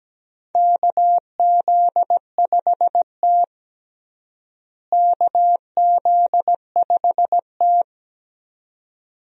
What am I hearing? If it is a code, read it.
Morse code, heard as a keyed tone, about 17 wpm, 700 Hz. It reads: KZ5T KZ5T
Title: USA Call Signs - 4 Characters - Repeated 2x - 17wpm